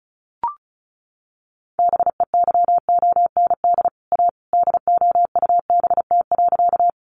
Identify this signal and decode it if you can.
Morse code, heard as a keyed tone, about 34 wpm, 700 Hz. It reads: BEYOND A DOUBT.
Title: Sentences from Top 1500 Words 35wpm